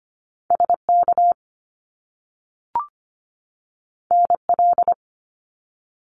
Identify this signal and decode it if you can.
Morse code, heard as a keyed tone, about 25 wpm, 700 Hz. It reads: SK NL
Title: Canadian Provinces and Territories 25wpm